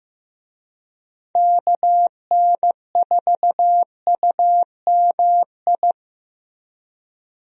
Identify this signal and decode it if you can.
Morse code, heard as a keyed tone, about 16 wpm, 700 Hz. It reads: KN4UMI
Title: USA Call Signs - 6 Characters - Repeated 3x - 15wpm